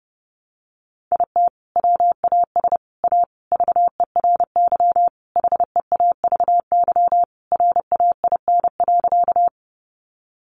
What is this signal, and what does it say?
Morse code, heard as a keyed tone, about 30 wpm, 700 Hz. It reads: IT WAS A VERY HEAVY RAIN.
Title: Sentences from Top 1100 Words 30wpm